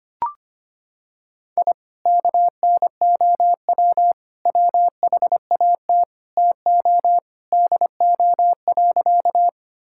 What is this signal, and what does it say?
Morse code, heard as a keyed tone, about 25 wpm, 700 Hz. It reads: I KNOW WHAT TO DO.
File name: practice / 25wpm / Sentences with 3-Word Phrases - 25wpm